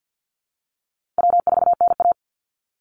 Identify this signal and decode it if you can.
Morse code, heard as a keyed tone, about 50 wpm, 700 Hz. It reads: W4NA
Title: Easy Call Signs - 50wpm